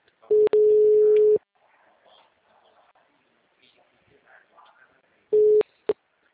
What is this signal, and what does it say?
Ringback tone (Europe, 425 Hz).
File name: out- -2014